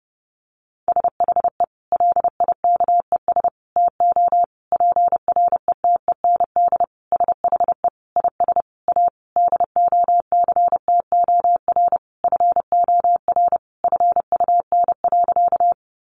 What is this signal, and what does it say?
Morse code, heard as a keyed tone, about 30 wpm, 700 Hz. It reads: SHE LIKES TO PRETEND SHE IS A DOCTOR FOR FUN.
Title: Additional Easy Phrases 30wpm